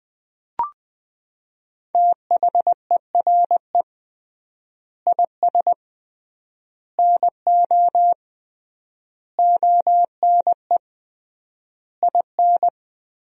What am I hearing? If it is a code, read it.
Morse code, heard as a keyed tone, about 20 wpm, 700 Hz. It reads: THERE IS NO ONE IN
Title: 3x Word-Spacing - Sentences from Top 500 Words - 20wpm